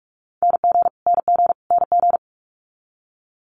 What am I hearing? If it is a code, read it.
Morse code, heard as a keyed tone, about 46 wpm, 700 Hz. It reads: NG NG NG
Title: Multi-letter Phonemes Repeated 3x 45wpm